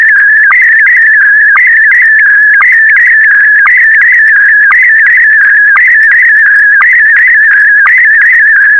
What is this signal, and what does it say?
SSTV (slow-scan television), Scottie DX.